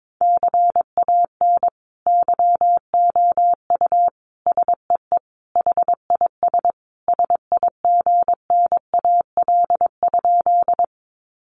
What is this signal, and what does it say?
Morse code, heard as a keyed tone, about 22 wpm, 700 Hz. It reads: CAN YOU SEE HIS SIGNAL?
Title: Sentences from Top 1400 Words 22wpm